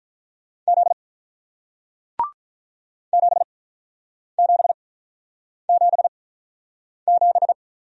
Morse code, heard as a keyed tone, about 48 wpm, spaced as for 9 wpm, 700 Hz. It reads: Z 7777